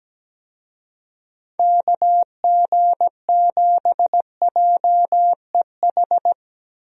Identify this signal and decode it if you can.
Morse code, heard as a keyed tone, about 17 wpm, 700 Hz. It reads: KG7JEH
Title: USA Call Signs - 6 Characters - 17wpm